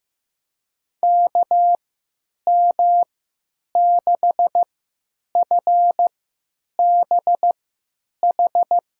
Morse code, heard as a keyed tone, about 15 wpm, spaced as for 8 wpm, 700 Hz. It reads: KM6FBH
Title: USA Call Signs - 6 Characters - 3x Character Spacing - 15wpm